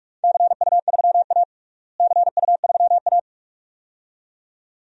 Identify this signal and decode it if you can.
Morse code, heard as a keyed tone, about 44 wpm, 700 Hz. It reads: CU3A CU3A
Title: DXCC-100 Call Signs - 4 Characters - Repeated 2x - 45wpm